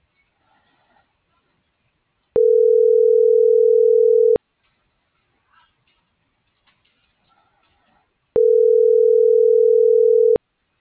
Ringback tone (North America), 440 + 480 Hz.